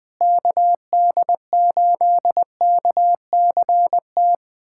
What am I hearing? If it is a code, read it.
Morse code, heard as a keyed tone, about 20 wpm, 700 Hz. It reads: KD8KCT
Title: USA Callsign DE Callsign - 6 Characters - 20wpm